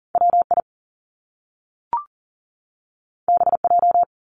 Morse code, heard as a keyed tone, about 39 wpm, 700 Hz. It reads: WI BJ
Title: Top Two Letter Combinations in Words 40wpm